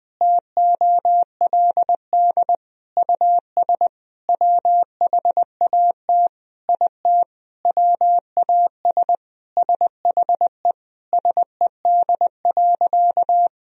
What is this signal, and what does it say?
Morse code, heard as a keyed tone, about 20 wpm, 700 Hz. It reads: TOLD US WHAT IT WAS SHE SED.
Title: Sentences from Top 400 Words and Abbreviations 20wpm